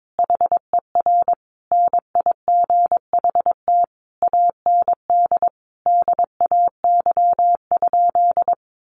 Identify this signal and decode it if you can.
Morse code, heard as a keyed tone, about 22 wpm, 700 Hz. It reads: HER NIGHT AND DAY?
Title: Sentences from Top 1100 Words 22wpm